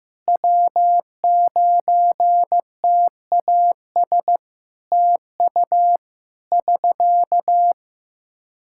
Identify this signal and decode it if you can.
Morse code, heard as a keyed tone, about 15 wpm, 700 Hz. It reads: W9TAS TU <SK>
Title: Practice QSO - Exchange 4 - Ending - 15wpm